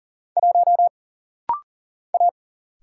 Morse code, heard as a keyed tone, about 40 wpm, 700 Hz. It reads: 1 A
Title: Single Letter-Number - Rapid-Fire 40wpm